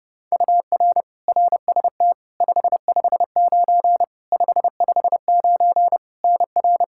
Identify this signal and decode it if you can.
Morse code, heard as a keyed tone, about 32 wpm, 700 Hz. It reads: UR RST 559 559 NR
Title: Practice QSO - Exchange 1 - Beginning - 30wpm